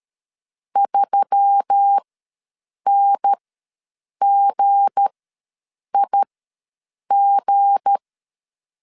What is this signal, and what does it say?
Morse code, heard as a keyed tone, about 13 wpm, spaced as for 7 wpm, 800 Hz. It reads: 3NGIG
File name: LEZIONI / 7 GF9Z / 7 G ws 12wpm 3140